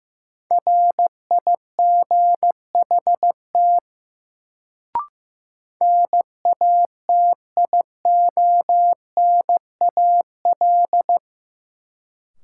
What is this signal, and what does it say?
Morse code, heard as a keyed tone, about 15 wpm, 700 Hz. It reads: RIGHT NATIONAL
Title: Sets of 2 Words - New Word Drill - 15wpm